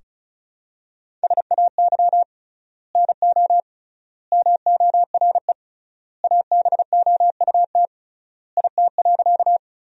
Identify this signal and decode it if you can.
Morse code, heard as a keyed tone, about 36 wpm, 700 Hz. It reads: SAY NO MORE ABOUT IT.
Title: 3x Word-Spacing - Sentences from Top 200 Words - 35wpm